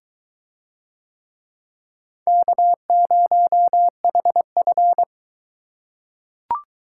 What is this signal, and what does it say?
Morse code, heard as a keyed tone, about 20 wpm, 700 Hz. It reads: K0HF